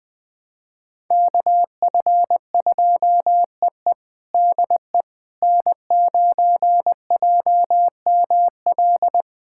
Morse code, heard as a keyed tone, about 20 wpm, 700 Hz. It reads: KF2EE DE N9JML
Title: USA Callsign DE Callsign - 5 Characters - 20wpm